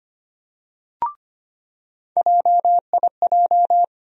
Morse code, heard as a keyed tone, about 25 wpm, 700 Hz. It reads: JIJ
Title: USA Call Sign Postfixes - 3 Letters 25wpm